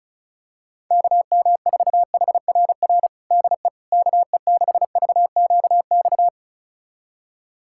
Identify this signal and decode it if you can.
Morse code, heard as a keyed tone, about 36 wpm, 700 Hz. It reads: KM4HRR DE KE6VQX
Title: USA Callsign DE Callsign 35wpm